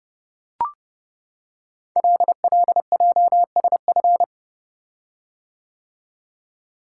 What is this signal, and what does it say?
Morse code, heard as a keyed tone, about 31 wpm, 690 Hz. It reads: LLJSF
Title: Code Groups - 5 Letters - 30wpm